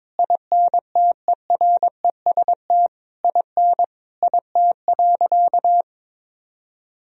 Morse code, heard as a keyed tone, about 22 wpm, 700 Hz. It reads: INTEREST IN IT.
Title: Sentences from Top 1000 Words - Review Part 2 - 22wpm